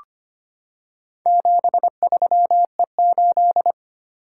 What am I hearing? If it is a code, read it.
Morse code, heard as a keyed tone, about 25 wpm, 700 Hz. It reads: 73E8
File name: practice / 25wpm / Four Letter-Number Combinations - 25wpm